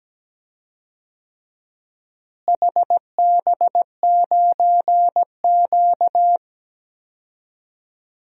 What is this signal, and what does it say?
Morse code, heard as a keyed tone, about 17 wpm, 700 Hz. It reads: HB9Q
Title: DXCC-100 Call Signs - 4 Characters - Repeated 2x - 17wpm